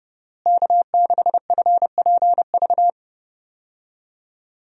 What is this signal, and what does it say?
Morse code, heard as a keyed tone, about 32 wpm, 700 Hz. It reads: K6FPV
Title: USA Call Signs - 5 Characters - Repeated 3x - 30wpm